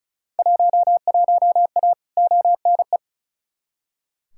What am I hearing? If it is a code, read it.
Morse code, heard as a keyed tone, about 36 wpm, 700 Hz. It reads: 11A ONE
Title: Field Day Exchanges Encore - 35wpm